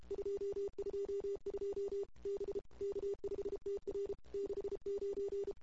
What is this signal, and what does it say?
Morse code, heard as a keyed tone, about 33 wpm, 440 Hz. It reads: 222 B K5TR 69
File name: CW / 2008-ss-cw-k5tr-op-n5rz -23:00